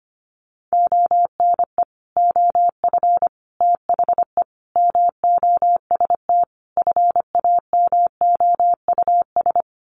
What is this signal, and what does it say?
Morse code, heard as a keyed tone, about 25 wpm, 700 Hz. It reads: ONE OF THE MOST FAMOUS
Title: Sets of 5 Words Encore2 25wpm